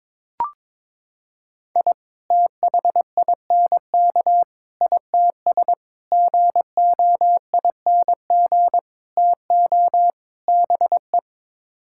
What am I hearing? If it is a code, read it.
Morse code, heard as a keyed tone, about 22 wpm, 700 Hz. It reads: I THINK ITS GOING TO BE
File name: practice / 22wpm / Sets of 6 Words - 22wpm